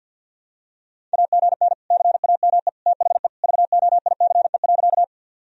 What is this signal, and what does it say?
Morse code, heard as a keyed tone, about 52 wpm, 700 Hz. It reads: AGN CAME THE VOICE.
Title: Sentences from Top 500 Words and Abbreviations 50wpm